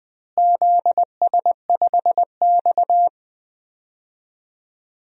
Morse code, heard as a keyed tone, about 20 wpm, 700 Hz. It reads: ZS5X
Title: DXCC-100 Call Signs - 4 Characters - Repeated 3x - 20wpm